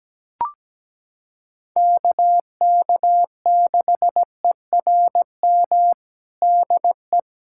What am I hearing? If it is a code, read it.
Morse code, heard as a keyed tone, about 17 wpm, 700 Hz. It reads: KK6ERM DE
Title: USA Callsign DE Callsign - 6 Characters - 17wpm